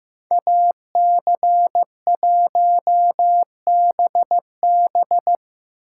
Morse code, heard as a keyed tone, about 17 wpm, 700 Hz. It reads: AC1BB
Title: Speed Racing - USA 2x2 Call Signs 15wpm